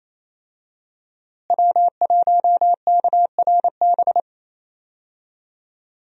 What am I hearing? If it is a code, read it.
Morse code, heard as a keyed tone, about 28 wpm, 700 Hz. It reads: W1KRB
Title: USA Call Signs - 5 Characters - Repeated 2x - 28wpm